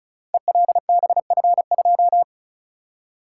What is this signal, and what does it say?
Morse code, heard as a keyed tone, about 36 wpm, 700 Hz. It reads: ELBF2
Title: Code Groups - 5 Letter:Numbers - 35wpm